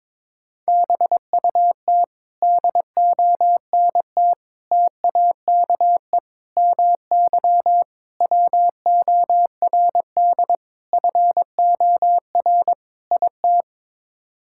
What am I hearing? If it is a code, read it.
Morse code, heard as a keyed tone, about 22 wpm, 700 Hz. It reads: BUT DONT TAKE MY WORD FOR IT
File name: practice / 22wpm / Sets of 7 Words - 22wpm